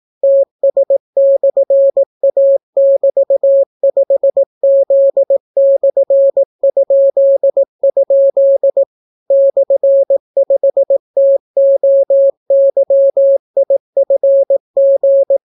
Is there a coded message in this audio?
Yes: Morse code from 0.2 to 15.4 s, heard as a keyed tone, about 18 wpm, 550 Hz. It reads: TS/A<BT>5Z/?? /5TOYIFG